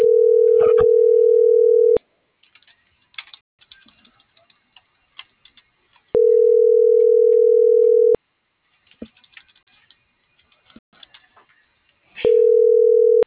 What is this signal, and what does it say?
Ringback tone (North America), 440 + 480 Hz.